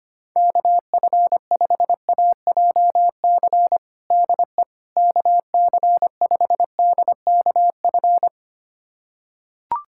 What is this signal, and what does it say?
Morse code, heard as a keyed tone, about 25 wpm, 700 Hz. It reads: KF5AJC DE KC5DKF
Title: USA Callsign DE Callsign - 6 Characters - 25wpm